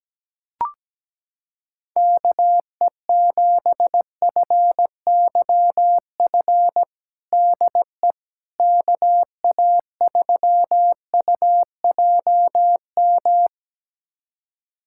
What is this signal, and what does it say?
Morse code, heard as a keyed tone, about 17 wpm, 700 Hz. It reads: KE7FYF DE KA3UJM
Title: USA Callsign DE Callsign - 6 Characters - 17wpm